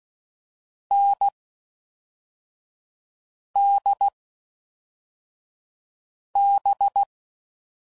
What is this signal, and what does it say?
Morse code, heard as a keyed tone, about 16 wpm, spaced as for 4 wpm, 800 Hz. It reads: NDB
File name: Mixed / Short Runs / NDB NDB6